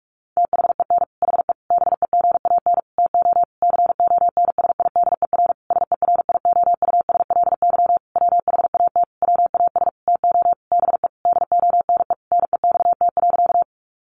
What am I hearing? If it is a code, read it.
Morse code, heard as a keyed tone, about 44 wpm, 700 Hz. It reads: THEN HE BEGAN TO CONSIDER SERIOUSLY WHAT WAS TO BE DONE NEXT.
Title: Sentences from Top 1200 Words 45wpm